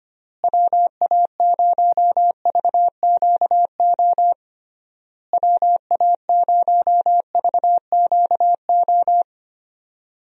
Morse code, heard as a keyed tone, about 25 wpm, 700 Hz. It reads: WA0VQO WA0VQO
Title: USA Call Signs - 6 Characters - Repeated 3x - 25wpm